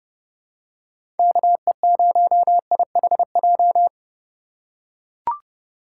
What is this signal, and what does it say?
Morse code, heard as a keyed tone, about 30 wpm, 700 Hz. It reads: KE0IHJ
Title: USA Call Signs - 6 Characters - 3x Character Spacing - 30wpm